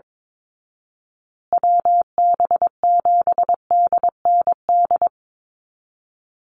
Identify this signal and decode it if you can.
Morse code, heard as a keyed tone, about 22 wpm, 700 Hz. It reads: WB7DND
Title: Call Signs Repeated 2x - 22wpm